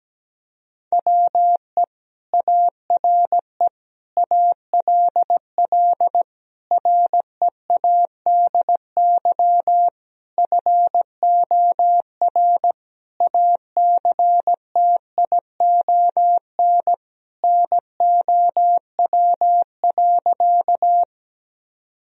Morse code, heard as a keyed tone, about 17 wpm, 700 Hz. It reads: WE ARE ALL READY FOR ACTION NOW.
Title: Sentences from Top 1000 Words - Review Part 4 - 17wpm